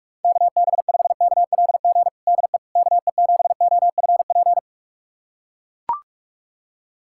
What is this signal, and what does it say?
Morse code, heard as a keyed tone, about 42 wpm, 700 Hz. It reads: KB5KLG DE KE7OFP
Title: USA Callsign DE Callsign 45wpm